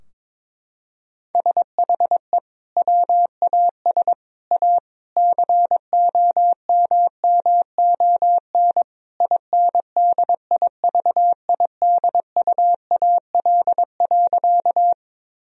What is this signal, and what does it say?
Morse code, heard as a keyed tone, about 22 wpm, 700 Hz. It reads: SHE WAS A COMMON INDIVIDUAL.
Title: Sentences from Top 1000 Words - Review Part 4 - 22wpm